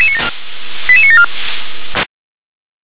Five-tone selective call (ZVEI), heard as quick runs of five tones, tones of about 70 ms. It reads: 88974
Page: /357/SDR/ZVEI/